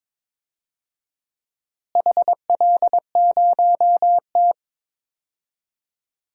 Morse code, heard as a keyed tone, about 22 wpm, 700 Hz. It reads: HL0T
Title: DXCC-100 Call Signs - 4 Characters - Repeated 3x - 22wpm